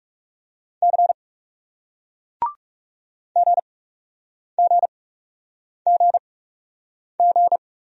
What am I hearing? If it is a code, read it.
Morse code, heard as a keyed tone, about 40 wpm, 700 Hz. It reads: C G G G G